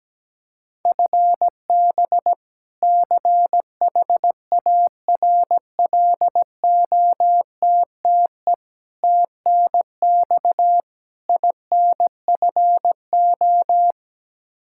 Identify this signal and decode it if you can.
Morse code, heard as a keyed tone, about 17 wpm, 700 Hz. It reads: FB CHARLOTTE TNX INFO
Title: Practice QSO - Exchange 3 - More Info - 17wpm